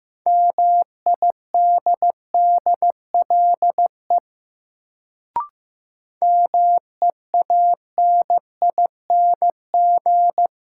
Morse code, heard as a keyed tone, about 15 wpm, 700 Hz. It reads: MIDDLE MEANING
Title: Sentences from Top 900 Words - New Word Drill - 15wpm